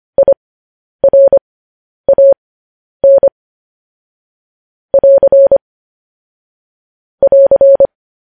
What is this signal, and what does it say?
Morse code, heard as a keyed tone, about 25 wpm, spaced as for 10 wpm, 550 Hz. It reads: IRAN <AR> <AR>